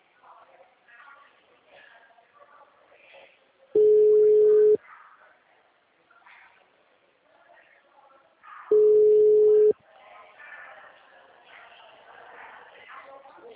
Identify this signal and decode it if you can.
Ringback tone (Europe, 425 Hz), with speech.